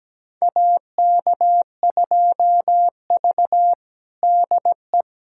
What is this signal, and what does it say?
Morse code, heard as a keyed tone, about 17 wpm, 700 Hz. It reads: AK2V DE
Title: USA Callsign DE Callsign - 4 Characters - 17wpm